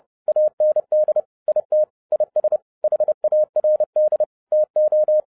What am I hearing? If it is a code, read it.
Morse code, heard as a keyed tone, about 30 wpm, 610 Hz. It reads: AND IT IS HARD TO
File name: 2011-07-mp3 / n20 GREETINGS FROM SINGAPORE 30wpm